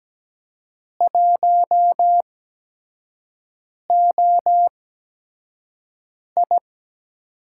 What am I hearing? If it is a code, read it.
Morse code, heard as a keyed tone, about 17 wpm, spaced as for 5 wpm, 700 Hz. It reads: 1OI
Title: USA Call Signs - 8x Character Spacing - 17wpm